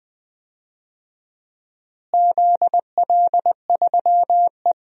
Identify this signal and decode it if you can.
Morse code, heard as a keyed tone, about 21 wpm, 700 Hz. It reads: ZL3E
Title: DXCC-100 Call Signs - 4 Characters - Repeated 3x - 20wpm